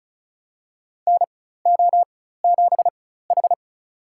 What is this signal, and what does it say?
Morse code, heard as a keyed tone, about 36 wpm, spaced as for 16 wpm, 700 Hz. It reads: NO7H
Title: USA Call Signs - 4 Characters - 4x Character Spacing - 35wpm